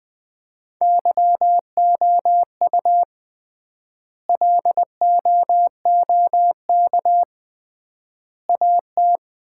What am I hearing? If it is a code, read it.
Morse code, heard as a keyed tone, about 20 wpm, 700 Hz. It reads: YOU LOOK AT
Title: 3x Word-Spacing - Sets of 3 Words - 20wpm